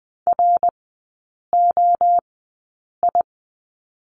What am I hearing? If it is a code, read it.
Morse code, heard as a keyed tone, about 20 wpm, spaced as for 8 wpm, 700 Hz. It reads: ROI
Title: 053 Lesson 10e - Letter r - Three Characters 20wpm